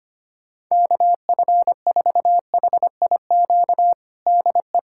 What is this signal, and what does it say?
Morse code, heard as a keyed tone, about 25 wpm, 700 Hz. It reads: KF4HIQ DE
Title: USA Callsign DE Callsign 25wpm